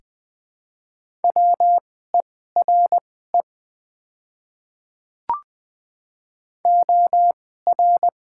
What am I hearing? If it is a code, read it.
Morse code, heard as a keyed tone, about 20 wpm, spaced as for 14 wpm, 700 Hz. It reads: WERE OR
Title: 2x Character Spacing - Top 200 Words - 20wpm